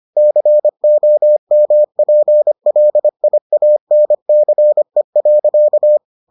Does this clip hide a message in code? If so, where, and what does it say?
Yes: Morse code from 0.2 to 6.0 s, heard as a keyed tone, about 25 wpm, 600 Hz. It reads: COMPLIANCE.